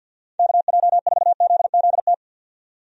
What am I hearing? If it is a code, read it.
Morse code, heard as a keyed tone, about 47 wpm, 700 Hz. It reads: KJ4ZZT
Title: USA Call Signs - 6 Characters - 50wpm